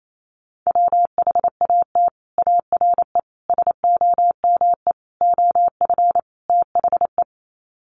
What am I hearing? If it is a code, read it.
Morse code, heard as a keyed tone, about 28 wpm, 690 Hz. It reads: WHAT ARE SOME OF THE
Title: Sets of 5 Words Encore 28wpm